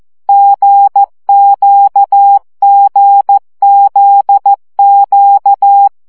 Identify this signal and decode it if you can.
Morse code, heard as a keyed tone, about 14 wpm, 800 Hz. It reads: GQGZQ
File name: CW gyakorlas g z q 12 baud